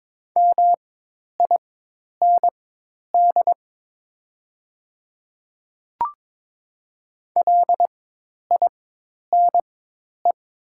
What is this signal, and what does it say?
Morse code, heard as a keyed tone, about 22 wpm, spaced as for 10 wpm, 700 Hz. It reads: MIND LINE
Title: 4x Character Spacing - Top 500 Words - 22wpm